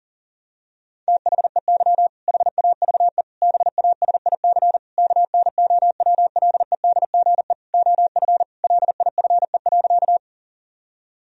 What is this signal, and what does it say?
Morse code, heard as a keyed tone, about 40 wpm, 700 Hz. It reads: THEY HAVE BASIC KNOWLEDGE OF LIFE.
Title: Sentences from Top 1000 Words - Review Part 3 - 40wpm